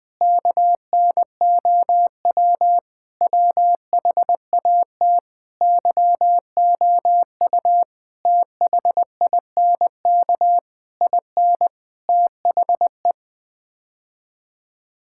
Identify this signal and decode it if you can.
Morse code, heard as a keyed tone, about 19 wpm, 700 Hz. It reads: KNOW WHAT YOU THINK IN THE
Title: Sets of 8 Words - 20wpm